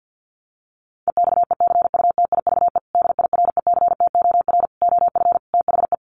Morse code, heard as a keyed tone, about 50 wpm, 700 Hz. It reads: EXECUTIVE DIRECTOR OF THE
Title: Sets of 5 Words Encore2 50wpm